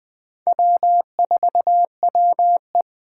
Morse code, heard as a keyed tone, about 24 wpm, 700 Hz. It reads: W4WE
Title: Speed Racing - USA 1x2 Call Signs 20wpm